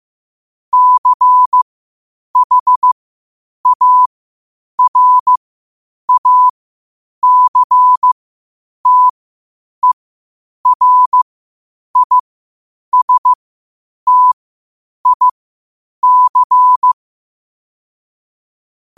Morse code, heard as a keyed tone, about 15 wpm, spaced as for 8 wpm, 1000 Hz. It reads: CHARACTERISTIC
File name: qotdfiles / 8wpm